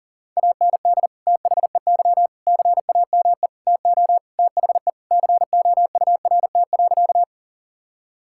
Morse code, heard as a keyed tone, about 40 wpm, 700 Hz. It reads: AND THEY CAME TO THE COURT.
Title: Sentences from Top 1600 Words 40wpm